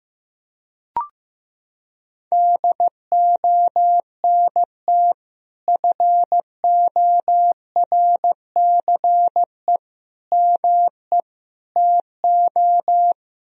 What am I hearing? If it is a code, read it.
Morse code, heard as a keyed tone, about 15 wpm, 700 Hz. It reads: DONT FORCE ME TO
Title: Sentences from Top 1000 Words - Review Part 3 - 15wpm